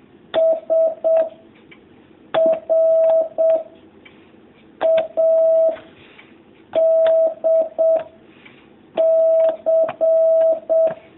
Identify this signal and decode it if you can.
Morse code, heard as a keyed tone, about 7 wpm, 650 Hz. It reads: S R A D C
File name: jh8cbh / 2019cw